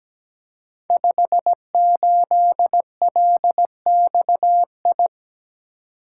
Morse code, heard as a keyed tone, about 17 wpm, 690 Hz. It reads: 58LXI
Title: Code Groups - 5 Letter:Numbers - 17wpm